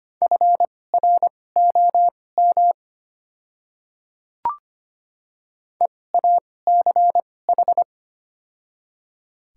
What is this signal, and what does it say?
Morse code, heard as a keyed tone, about 25 wpm, spaced as for 18 wpm, 700 Hz. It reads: FROM EACH
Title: 2x Character Spacing - Top 100 Words - 25wpm